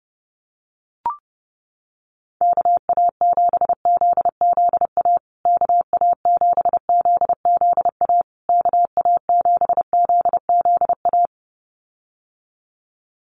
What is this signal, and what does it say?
Morse code, heard as a keyed tone, about 30 wpm, 700 Hz. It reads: KA7ZZA KA7ZZA KA7ZZA
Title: USA Call Signs Repeated 3x 30wpm